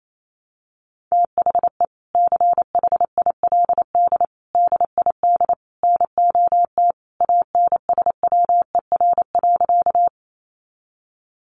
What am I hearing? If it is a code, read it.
Morse code, heard as a keyed tone, about 28 wpm, 700 Hz. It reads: THE CHILD DID NOT ANSWER.
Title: Sentences from Top 800 Words 28wpm